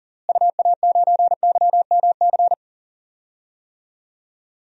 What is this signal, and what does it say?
Morse code, heard as a keyed tone, about 40 wpm, 700 Hz. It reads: UA9YMC
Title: DXCC-100 Call Signs - 6 Characters - Repeated 2x - 40wpm